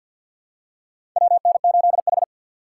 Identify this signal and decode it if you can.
Morse code, heard as a keyed tone, about 53 wpm, 690 Hz. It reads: WN8H
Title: USA Call Signs - 4 Characters - 6x Character Spacing - 50wpm